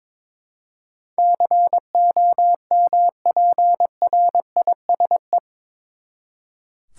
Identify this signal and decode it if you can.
Morse code, heard as a keyed tone, about 22 wpm, 700 Hz. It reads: COMPRISE
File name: practice / 22wpm / Eight Letter Words - 22wpm